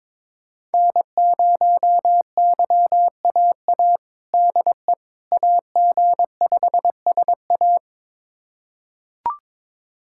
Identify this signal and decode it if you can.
Morse code, heard as a keyed tone, about 22 wpm, 700 Hz. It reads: N0YAA DE AG5SA
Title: USA Callsign DE Callsign 22wpm